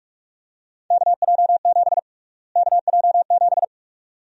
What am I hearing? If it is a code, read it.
Morse code, heard as a keyed tone, about 46 wpm, 700 Hz. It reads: KJ7 KJ7
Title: USA Call Sign Prefixes - Repeated 2x - 45wpm